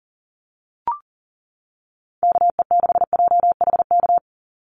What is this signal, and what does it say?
Morse code, heard as a keyed tone, about 40 wpm, 700 Hz. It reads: KE6JHK
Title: USA Call Signs 40wpm